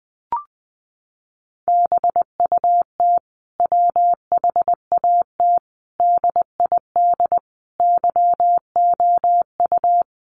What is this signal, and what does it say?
Morse code, heard as a keyed tone, about 19 wpm, 700 Hz. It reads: BUT WHAT DID YOU
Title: Sentences from Top 200 Words 20wpm